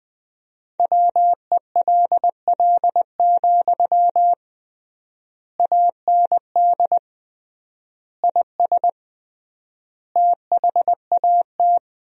Morse code, heard as a keyed tone, about 20 wpm, 700 Hz. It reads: WELL, AND IS THAT
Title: 3x Word-Spacing - Sentences from Top 200 Words - 20wpm